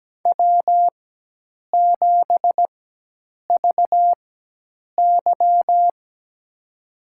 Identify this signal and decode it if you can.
Morse code, heard as a keyed tone, about 17 wpm, spaced as for 8 wpm, 700 Hz. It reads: W7VY
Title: USA Call Signs - 4 Characters - 4x Character Spacing - 17wpm